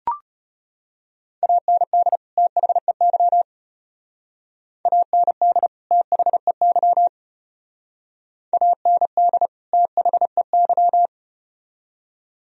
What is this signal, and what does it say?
Morse code, heard as a keyed tone, about 32 wpm, 700 Hz. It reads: AND THEY AND THEY AND THEY